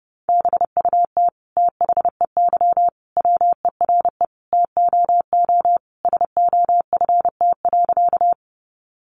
Morse code, heard as a keyed tone, about 30 wpm, 700 Hz. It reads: BUT THEY WERE TOO SOFT.
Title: Sentences from Top 1500 Words 30wpm